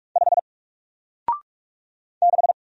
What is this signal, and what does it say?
Morse code, heard as a keyed tone, about 44 wpm, 700 Hz. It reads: 5 6
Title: Single Letter-Number Alt 45wpm